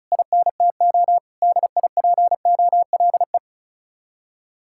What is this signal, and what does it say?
Morse code, heard as a keyed tone, about 46 wpm, 700 Hz. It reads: INTO DIPOLE